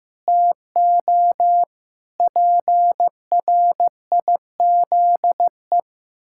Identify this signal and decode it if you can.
Morse code, heard as a keyed tone, about 15 wpm, 700 Hz. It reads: TO PRIZE
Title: Pangram Challenge v2 - 15wpm